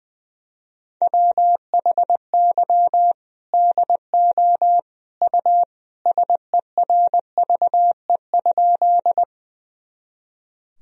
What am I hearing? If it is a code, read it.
Morse code, heard as a keyed tone, about 21 wpm, 700 Hz. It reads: WHY DO U SERVE?
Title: Sentences from Top 400 Words and Abbreviations 20wpm